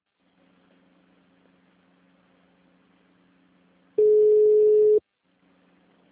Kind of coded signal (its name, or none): telephone call progress tone